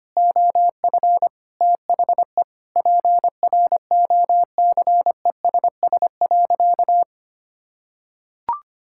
Morse code, heard as a keyed tone, about 25 wpm, 700 Hz. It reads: OF THE PROCESS.
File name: practice / 25wpm / Sentences with 2-Word Phrases - 25wpm